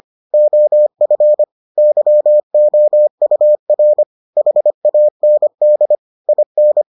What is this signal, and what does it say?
Morse code, heard as a keyed tone, about 25 wpm, 600 Hz. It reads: OF YOUR HAND IN